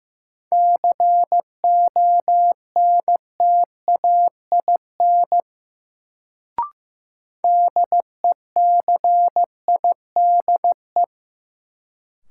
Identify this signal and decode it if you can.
Morse code, heard as a keyed tone, about 16 wpm, 700 Hz. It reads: CONTAIN DECIDE
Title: Sentences from Top 500 Words - New Word Drill - 15wpm